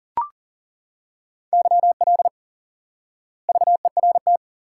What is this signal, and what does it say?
Morse code, heard as a keyed tone, about 40 wpm, 700 Hz. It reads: YL VERT
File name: IQE Course 3 - Lesson 89 - YAGI - Two Elements - 40wpm